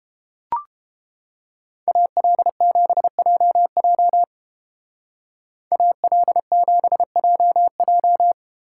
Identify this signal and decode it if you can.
Morse code, heard as a keyed tone, about 31 wpm, 700 Hz. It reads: AL7JJ AL7JJ